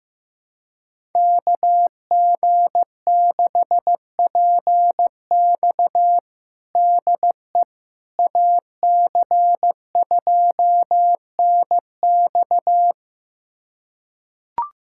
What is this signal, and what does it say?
Morse code, heard as a keyed tone, about 16 wpm, 700 Hz. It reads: KG6PX DE AC2NX
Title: USA Callsign DE Callsign - 5 Characters - 15wpm